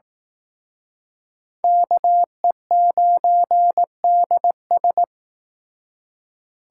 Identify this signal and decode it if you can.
Morse code, heard as a keyed tone, about 19 wpm, 700 Hz. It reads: KE9DS